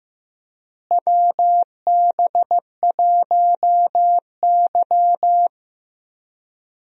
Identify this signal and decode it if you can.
Morse code, heard as a keyed tone, about 15 wpm, 700 Hz. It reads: WB1Y
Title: USA Call Signs - 4 Characters - Repeated 2x - 15wpm